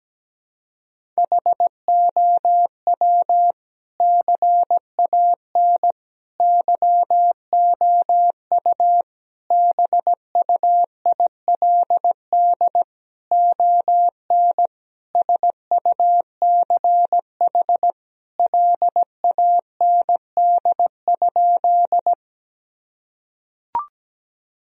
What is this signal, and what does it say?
Morse code, heard as a keyed tone, about 17 wpm, 700 Hz. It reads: HOW CAN YOU BUILD ON SUCH LAND?
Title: Sentences from Top 1000 Words - Review Part 1 - 17wpm